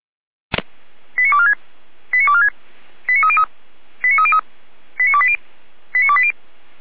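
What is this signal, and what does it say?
Five-tone selective call (ZVEI), heard as quick runs of five tones, tones of about 70 ms. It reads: 89246 89246 89303 89303 89280 89280
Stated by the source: ZVEI five-tone selective call received on SDR.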